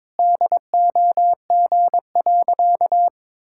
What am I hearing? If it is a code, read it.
Morse code, heard as a keyed tone, about 22 wpm, 700 Hz. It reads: DOG.
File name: practice / 22wpm / Sentences with 8-Word Phrases - 22wpm